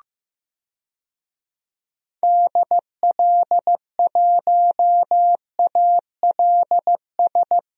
Morse code, heard as a keyed tone, about 15 wpm, 700 Hz. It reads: DL1ALS
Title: DXCC-100 Call Signs - 6 Characters - Repeated 3x - 15wpm